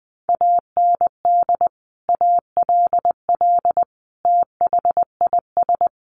Morse code, heard as a keyed tone, about 19 wpm, 700 Hz. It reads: AND ALL THIS
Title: Sentences from Top 1200 Words 20wpm